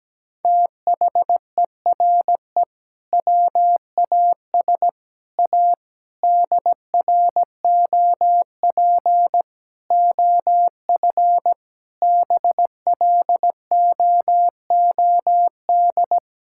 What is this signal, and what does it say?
Morse code, heard as a keyed tone, about 17 wpm, 700 Hz. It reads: THERE WAS A DROP OF BLOOD
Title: Sentences from Top 1600 Words 17wpm